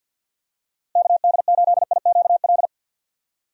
Morse code, heard as a keyed tone, about 52 wpm, 700 Hz. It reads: KD8IQL
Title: USA Call Signs - 6 Characters - Repeated 3x - 50wpm